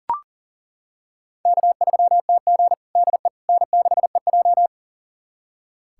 Morse code, heard as a keyed tone, about 40 wpm, 700 Hz. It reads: K3TG DE N6EJ
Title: USA Callsign DE Callsign - 4 Characters - 40wpm